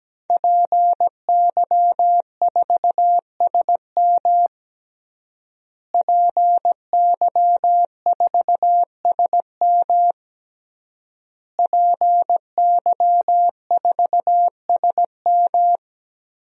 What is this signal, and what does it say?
Morse code, heard as a keyed tone, about 17 wpm, 700 Hz. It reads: PY4SM PY4SM PY4SM
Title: DXCC-100 Call Signs - 5 Characters - Repeated 3x - 17wpm